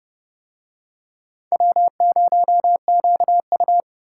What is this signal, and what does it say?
Morse code, heard as a keyed tone, about 30 wpm, 700 Hz. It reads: W0QU